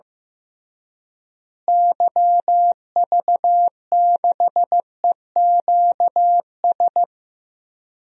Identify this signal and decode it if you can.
Morse code, heard as a keyed tone, about 16 wpm, 700 Hz. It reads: YV6EQS
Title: DXCC-100 Call Signs - 6 Characters - Repeated 3x - 15wpm